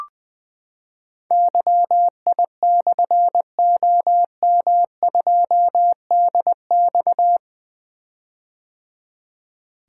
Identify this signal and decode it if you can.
Morse code, heard as a keyed tone, about 20 wpm, 700 Hz. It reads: YI/OM2DX
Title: Difficult Call Signs 20wpm